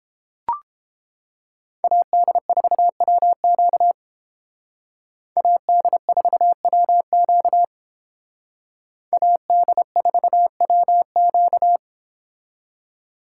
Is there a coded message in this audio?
Yes: Morse code from 1.8 to 11.8 s, heard as a keyed tone, about 31 wpm, 700 Hz. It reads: AD4WQ AD4WQ AD4WQ